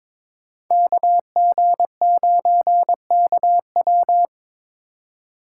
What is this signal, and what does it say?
Morse code, heard as a keyed tone, about 23 wpm, 700 Hz. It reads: KG9KW
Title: USA Call Signs - 5 Characters - 2x Character Spacing - 22wpm